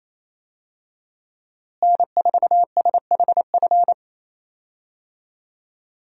Morse code, heard as a keyed tone, about 28 wpm, 700 Hz. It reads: N4SHF
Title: USA Call Signs - 5 Characters - Repeated 2x - 28wpm